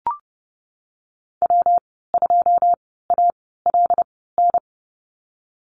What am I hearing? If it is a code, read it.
Morse code, heard as a keyed tone, about 30 wpm, spaced as for 17 wpm, 700 Hz. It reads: W2ALN
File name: Lesson 19g - Letter w - Call Signs 30wpm